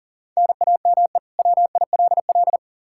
Morse code, heard as a keyed tone, about 40 wpm, 700 Hz. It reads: NAME WILL
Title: K1USN SST QSO Practice 40wpm